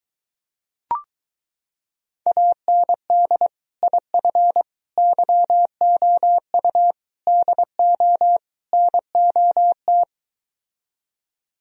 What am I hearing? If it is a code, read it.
Morse code, heard as a keyed tone, about 23 wpm, 700 Hz. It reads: AND IF YOU DO NOT